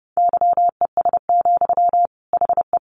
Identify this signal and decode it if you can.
Morse code, heard as a keyed tone, about 30 wpm, 700 Hz. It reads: YES, HE
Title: Sentences from Top 900 Words 30wpm